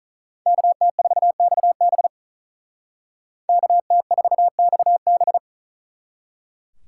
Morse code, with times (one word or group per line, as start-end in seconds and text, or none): 0.5-2.1 KT4XB
3.5-5.4 KT4XB